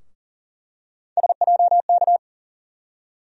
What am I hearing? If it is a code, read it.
Morse code, heard as a keyed tone, about 40 wpm, 700 Hz. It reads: SJK
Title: USA Call Sign Postfixes - 3 Letters 40wpm